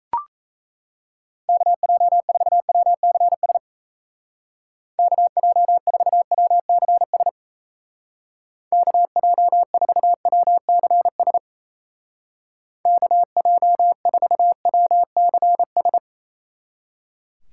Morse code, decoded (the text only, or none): KJ4WCS KJ4WCS KJ4WCS KJ4WCS